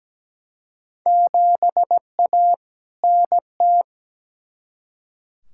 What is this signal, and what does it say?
Morse code, heard as a keyed tone, about 17 wpm, 700 Hz. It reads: 7A NT
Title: Field Day Exchanges Encore - 17wpm